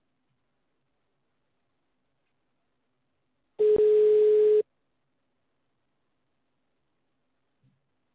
Ringback tone (Europe, 425 Hz).